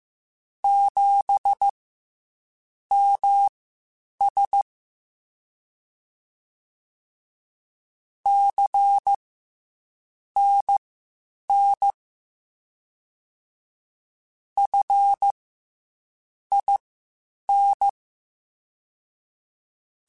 Morse code, heard as a keyed tone, about 15 wpm, spaced as for 7 wpm, 800 Hz. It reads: 7MS CNN FIN
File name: CW Comparisons / CW Comparisons-mp3s / CW Comparisons Broken Characters @13